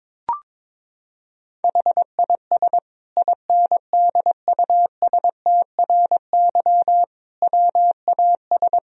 Morse code, heard as a keyed tone, about 22 wpm, 700 Hz. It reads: HIS INDUSTRY WAS
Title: Sentences from Top 1000 Words - Review Part 4 - 22wpm